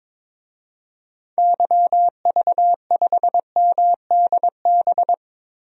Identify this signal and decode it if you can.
Morse code, heard as a keyed tone, about 22 wpm, 700 Hz. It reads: YV5MDB
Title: DXCC-100 Call Signs - 6 Characters - Repeated 2x - 22wpm